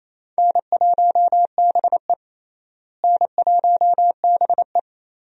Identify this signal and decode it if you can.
Morse code, heard as a keyed tone, about 28 wpm, 700 Hz. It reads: N1BE N1BE
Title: USA Call Signs - 4 Characters - Repeated 3x - 28wpm